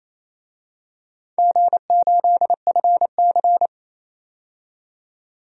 Morse code, heard as a keyed tone, about 28 wpm, 700 Hz. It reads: G8FC
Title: DXCC-100 Call Signs - 4 Characters - 28wpm